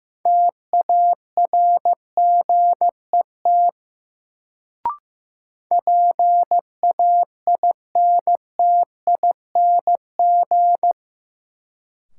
Morse code, heard as a keyed tone, about 16 wpm, 700 Hz. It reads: TARGET PAINTING
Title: Sentences from Top 1200 Words - New Word Drill Alt - 15wpm